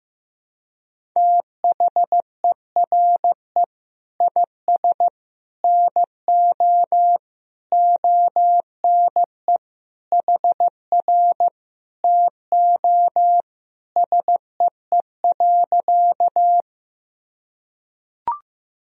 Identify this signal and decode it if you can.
Morse code, heard as a keyed tone, about 15 wpm, 700 Hz. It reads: THERE IS NO ONE HR TO SEE.
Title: Sentences from Top 200 Words and Abbreviations 15wpm